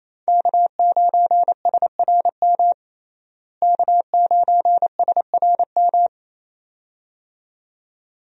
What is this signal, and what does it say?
Morse code, heard as a keyed tone, about 28 wpm, 700 Hz. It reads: K9SRM K9SRM
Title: USA Call Signs - 5 Characters - Repeated 2x - 28wpm